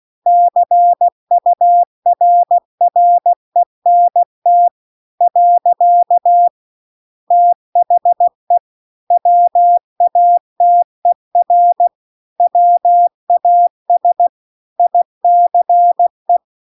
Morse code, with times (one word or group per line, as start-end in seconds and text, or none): 0.3-4.7 CURRENT
5.2-6.5 .
7.3-8.6 THE
9.1-11.9 WATER
12.4-14.3 WAS
14.8-16.4 ICE